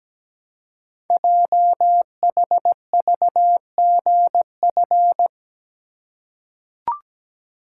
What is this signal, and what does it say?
Morse code, heard as a keyed tone, about 17 wpm, 700 Hz. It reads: JHVGF
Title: Code Groups - 5 Letters - 17wpm